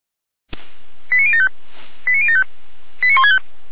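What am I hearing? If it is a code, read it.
Five-tone selective call (ZVEI), heard as quick runs of five tones, tones of about 70 ms. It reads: 89075 89075 89165
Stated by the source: Received on an SDR.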